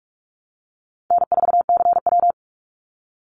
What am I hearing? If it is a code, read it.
Morse code, heard as a keyed tone, about 44 wpm, 700 Hz. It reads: N4CW
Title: Easy Call Signs Repeated 2x - 45wpm